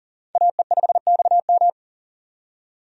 Morse code, heard as a keyed tone, about 40 wpm, 700 Hz. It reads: AE5XM
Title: USA Call Signs - 5 Characters - 40wpm